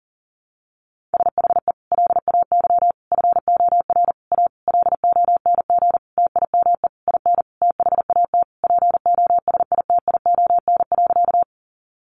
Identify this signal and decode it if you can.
Morse code, heard as a keyed tone, about 37 wpm, 700 Hz. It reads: SHE LAY FOR A LONG TIME IN THAT POSITION.
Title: Sentences from Top 900 Words 40wpm